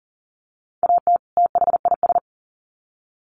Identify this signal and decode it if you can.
Morse code, heard as a keyed tone, about 40 wpm, 700 Hz. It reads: AT THIS
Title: Sets of 2 Words 40wpm